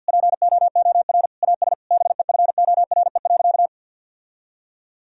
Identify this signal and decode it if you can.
Morse code, heard as a keyed tone, about 46 wpm, 700 Hz. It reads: POOR AS BEFORE.
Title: Sentences from Top 1000 Words - Review Part 2 - 50wpm